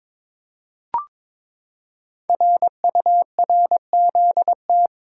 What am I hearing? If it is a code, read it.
Morse code, heard as a keyed tone, about 22 wpm, 700 Hz. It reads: RURZT
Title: Code Groups - 5 Letters - 22wpm